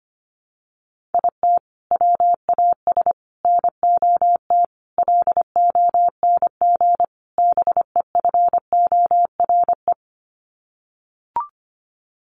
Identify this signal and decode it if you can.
Morse code, heard as a keyed tone, about 25 wpm, 700 Hz. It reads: IT WAS NOT LONG BEFORE
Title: Sets of 5 Words Encore2 25wpm